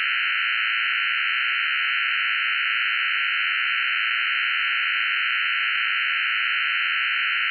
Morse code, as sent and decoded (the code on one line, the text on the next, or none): none